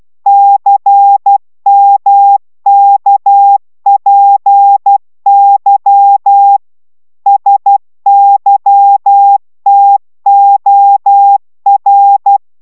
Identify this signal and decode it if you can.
Morse code, heard as a keyed tone, about 12 wpm, 800 Hz. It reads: CMKPY SYTOR
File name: taviro tanulas / cw 69 19 10